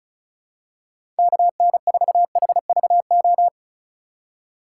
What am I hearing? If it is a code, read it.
Morse code, heard as a keyed tone, about 36 wpm, 700 Hz. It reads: KN4HVO
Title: USA Call Signs - 6 Characters - 4x Character Spacing - 35wpm